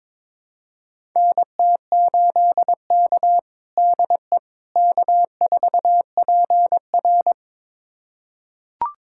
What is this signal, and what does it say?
Morse code, heard as a keyed tone, about 22 wpm, 700 Hz. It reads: NT8K DE K4PR
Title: USA Callsign DE Callsign - 4 Characters - 22wpm